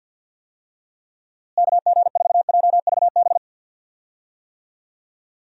Morse code, heard as a keyed tone, about 51 wpm, 700 Hz. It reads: KG4JVB
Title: USA Call Signs - 6 Characters - Repeated 2x - 50wpm